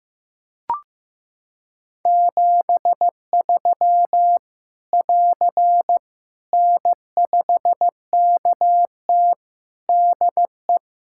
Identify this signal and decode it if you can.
Morse code, heard as a keyed tone, about 16 wpm, 700 Hz. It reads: 73 <AR> N5KT DE
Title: K1USN SST QSO Practice 15wpm